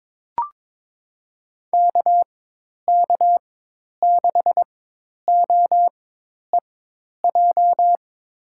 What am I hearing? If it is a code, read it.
Morse code, heard as a keyed tone, about 21 wpm, spaced as for 10 wpm, 700 Hz. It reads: KK6OEJ
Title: USA Call Signs - 4x Character Spacing - 22wpm